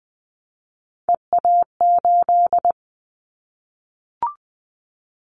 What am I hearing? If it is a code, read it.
Morse code, heard as a keyed tone, about 20 wpm, 700 Hz. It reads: EA8
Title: Callsign Prefixes for Top 100 DXCC - 20wpm